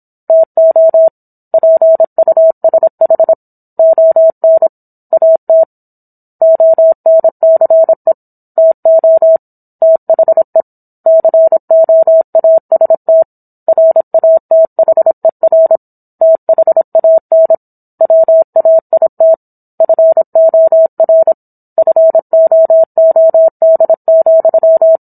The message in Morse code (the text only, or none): TO PUSH ON AT ONCE TO THE COAST RATHER THAN WAIT FOR FOOD,